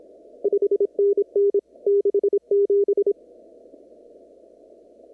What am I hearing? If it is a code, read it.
Morse code, heard as a keyed tone, about 26 wpm, 400 Hz. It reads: 5NN 67
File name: CW / kt5j wpx cw-op-k5pi